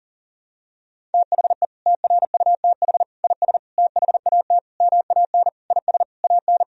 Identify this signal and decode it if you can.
Morse code, heard as a keyed tone, about 40 wpm, 700 Hz. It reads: THE TRUTH IS THAT MAN IS AN
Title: Sentences from Top 1000 Words - Review Part 1 - 40wpm